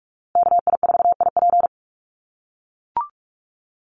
Morse code, heard as a keyed tone, about 44 wpm, 700 Hz. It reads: KI4IP
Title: USA Call Signs Repeated 3x 45wpm